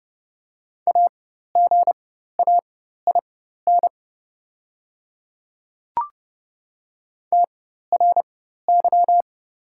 Morse code, heard as a keyed tone, about 30 wpm, spaced as for 14 wpm, 700 Hz. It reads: AGAIN TRY
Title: 4x Character Spacing - Top 300 Words - 30wpm